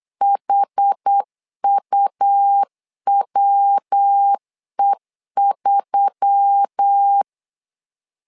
Morse code, heard as a keyed tone, about 8 wpm, 800 Hz. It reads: HUWE3
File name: LEZIONI / 4 Q1S3 8wpm 3037